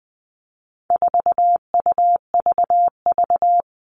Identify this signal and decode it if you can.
Morse code, heard as a keyed tone, about 20 wpm, 700 Hz. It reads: 4UVV
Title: All Cut Numbers - 4-Digit 20wpm